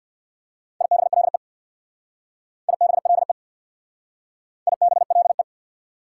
Morse code, heard as a keyed tone, about 59 wpm, 690 Hz. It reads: IBLE IBLE IBLE
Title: Speed-Racing Prefixes and Suffixes 50wpm